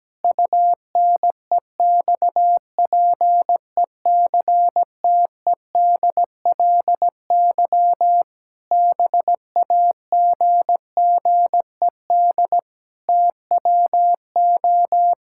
Morse code, heard as a keyed tone, about 17 wpm, 700 Hz. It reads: UNEXPECTEDLY BAGGED TWO
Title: Pangram Challenge v2 - 17wpm